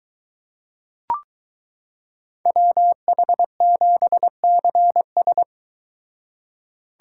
Morse code, heard as a keyed tone, about 23 wpm, 700 Hz. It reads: WH7CS